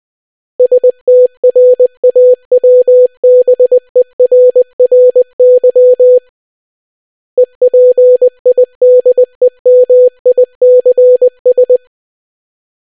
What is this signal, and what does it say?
Morse code, heard as a keyed tone, about 20 wpm, 500 Hz. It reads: STRAWBERRY EPIDEMICS